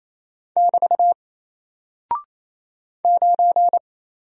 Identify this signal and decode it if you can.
Morse code, heard as a keyed tone, about 28 wpm, 700 Hz. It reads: <BT> 9
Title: Single Character (Letter:Number:Punctuation:Prosign) 28wpm